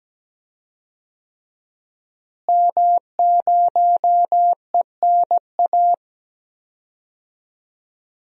Morse code, heard as a keyed tone, about 17 wpm, 700 Hz. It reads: M0ENA
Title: DXCC-100 Call Signs - 5 Characters - Repeated 3x - 17wpm